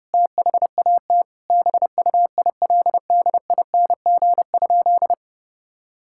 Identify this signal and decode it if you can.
Morse code, heard as a keyed tone, about 32 wpm, 700 Hz. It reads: THAT BUILDING?
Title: Sentences from Top 1000 Words - Review Part 5 - 30wpm